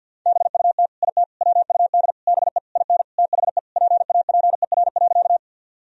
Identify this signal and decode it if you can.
Morse code, heard as a keyed tone, about 52 wpm, 700 Hz. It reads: BUT IT WUD BE IN THE PAPER.
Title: Sentences from Top 300 Words and Abbreviations 50wpm